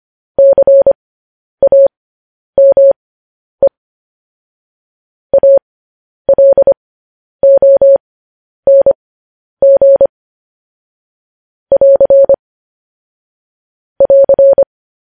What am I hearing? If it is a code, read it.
Morse code, heard as a keyed tone, about 25 wpm, spaced as for 10 wpm, 550 Hz. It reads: CAME ALONG <AR> <AR>
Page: news-headlines-in-morse-code-at-10-wpm